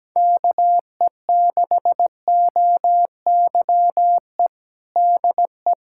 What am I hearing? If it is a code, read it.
Morse code, heard as a keyed tone, about 17 wpm, 700 Hz. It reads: KE6OYE DE
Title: Practice QSO - Exchange 4 - Ending - 17wpm